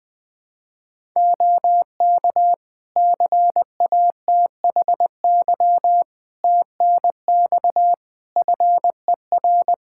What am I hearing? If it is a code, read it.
Morse code, heard as a keyed tone, about 19 wpm, 700 Hz. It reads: OK CATHY TNX FER
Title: Practice QSO - Exchange 4 - Ending - 20wpm